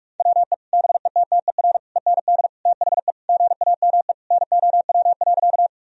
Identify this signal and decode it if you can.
Morse code, heard as a keyed tone, about 41 wpm, 700 Hz. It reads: WE BETTER END THE GAME NOW.
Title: Sentences from Top 1000 Words - Review Part 3 - 45wpm